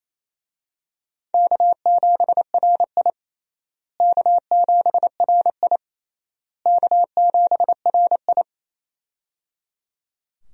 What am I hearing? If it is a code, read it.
Morse code, heard as a keyed tone, about 28 wpm, 700 Hz. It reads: K7RI K7RI K7RI
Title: USA Call Signs - 4 Characters - Repeated 3x - 28wpm